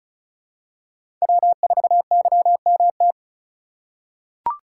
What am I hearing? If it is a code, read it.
Morse code, heard as a keyed tone, about 36 wpm, 700 Hz. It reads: W4YMT
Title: USA Call Signs - 3x Character Spacing - 35wpm